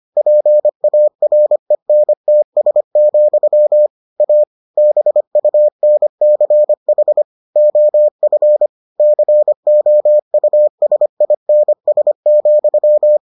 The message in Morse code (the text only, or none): PARENTS, A BUNCH OF COUSINS,